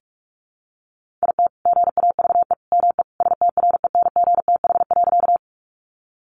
Morse code, heard as a keyed tone, about 44 wpm, 700 Hz. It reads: IT GAVE ME STRENGTH.
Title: Sentences from Top 1300 Words 45wpm